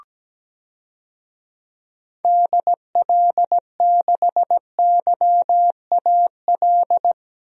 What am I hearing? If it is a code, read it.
Morse code, heard as a keyed tone, about 17 wpm, 700 Hz. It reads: DL6YAL
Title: DXCC-100 Call Signs - 6 Characters - Repeated 3x - 17wpm